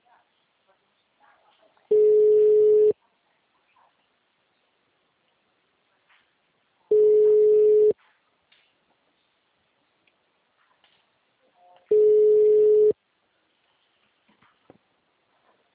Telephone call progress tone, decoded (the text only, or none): ringback tone